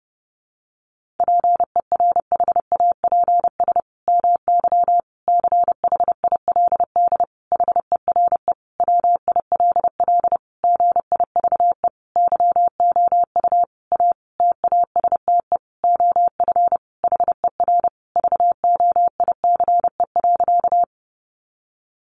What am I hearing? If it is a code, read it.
Morse code, heard as a keyed tone, about 30 wpm, 700 Hz. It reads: PERHAPS MY CHILD HERE WILL GIVE YOU A TASTE OF HER VOICE.
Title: Sentences from Top 1100 Words 30wpm